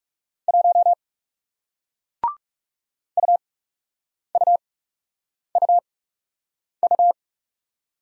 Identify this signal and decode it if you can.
Morse code, heard as a keyed tone, about 40 wpm, spaced as for 8 wpm, 700 Hz. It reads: 1 UUUU